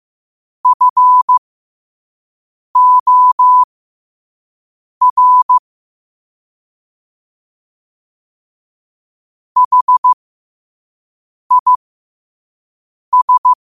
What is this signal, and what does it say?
Morse code, heard as a keyed tone, about 15 wpm, spaced as for 5 wpm, 1000 Hz. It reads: FOR HIS
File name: qotdfiles / 5wpm